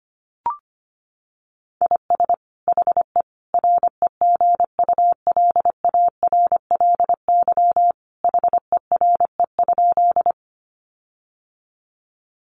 Morse code, heard as a keyed tone, about 25 wpm, 700 Hz. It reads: IS HE REGULARLY HERE?
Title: Sentences from Top 1400 Words 25wpm